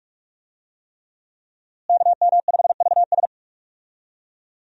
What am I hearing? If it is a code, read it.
Morse code, heard as a keyed tone, about 46 wpm, 700 Hz. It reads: KM5VS
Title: USA Call Signs - 5 Characters - Repeated 3x - 45wpm